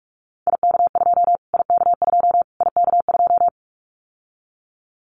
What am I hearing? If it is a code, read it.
Morse code, heard as a keyed tone, about 44 wpm, 700 Hz. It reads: IK2 IK2 IK2
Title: Callsign Prefixes for Top 25 DXCC - Repeated 3x - 45wpm